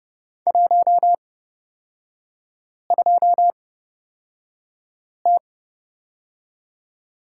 Morse code, heard as a keyed tone, about 30 wpm, spaced as for 5 wpm, 690 Hz. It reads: 12T
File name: Lesson 37e - Letter x - Three Characters 30wpm